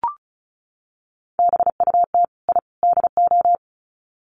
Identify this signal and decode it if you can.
Morse code, heard as a keyed tone, about 34 wpm, 700 Hz. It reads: BUT I DO
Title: Sets of 3 Words 35wpm